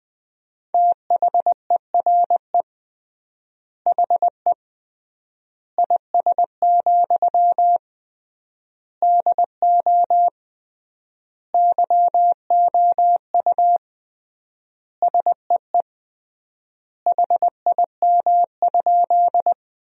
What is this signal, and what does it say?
Morse code, heard as a keyed tone, about 21 wpm, 700 Hz. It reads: THERE HE IS, DO YOU SEE HIM?
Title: 3x Word-Spacing - Sentences from Top 100 Words - 20wpm